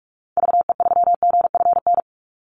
Morse code, heard as a keyed tone, about 44 wpm, 700 Hz. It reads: VE3GFN
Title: Call Signs - 45wpm